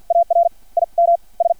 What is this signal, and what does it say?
Morse code, heard as a keyed tone, about 50 wpm, spaced as for 42 wpm, 680 Hz. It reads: AR IM S
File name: archived / CRSnet